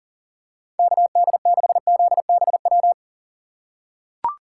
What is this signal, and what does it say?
Morse code, heard as a keyed tone, about 40 wpm, 700 Hz. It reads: KD6ZBW
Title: USA Call Signs - 6 Characters - 2x Character Spacing - 40wpm